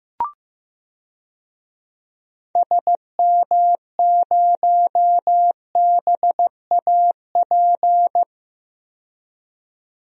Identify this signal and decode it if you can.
Morse code, heard as a keyed tone, about 15 wpm, 700 Hz. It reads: SM0BAP
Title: DXCC-100 Call Signs - 6 Characters - 15wpm